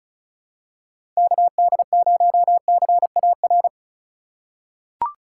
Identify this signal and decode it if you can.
Morse code, heard as a keyed tone, about 36 wpm, 700 Hz. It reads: KD0CAR
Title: USA Call Signs - 6 Characters - 4x Character Spacing - 35wpm